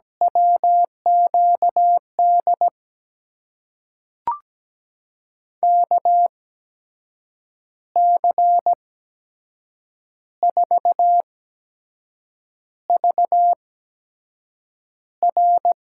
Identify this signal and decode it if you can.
Morse code, heard as a keyed tone, about 17 wpm, spaced as for 6 wpm, 700 Hz. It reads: WQD KC4VR
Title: USA Call Signs - 5 Characters - 8x Character Spacing - 17wpm